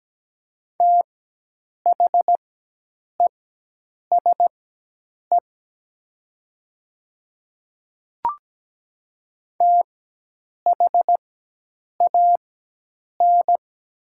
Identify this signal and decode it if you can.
Morse code, heard as a keyed tone, about 17 wpm, spaced as for 8 wpm, 700 Hz. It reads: THESE THAN
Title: 4x Character Spacing - Top 200 Words - 17wpm